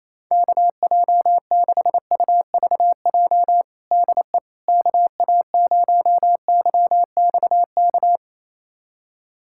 Morse code, heard as a keyed tone, about 28 wpm, 700 Hz. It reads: KJ6UVJ DE KA0YXK
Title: USA Callsign DE Callsign - 6 Characters - 28wpm